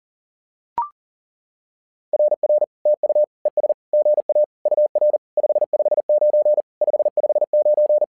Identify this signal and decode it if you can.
Morse code, heard as a keyed tone, about 40 wpm, 600 Hz. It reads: RR TU ES GA UR 559 559
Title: SOTA QSO Practice 40wpm